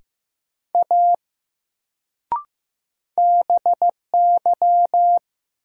Morse code, heard as a keyed tone, about 15 wpm, 700 Hz. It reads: A BY
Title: Sentences from Top 100 Words - New Word Drill - 15wpm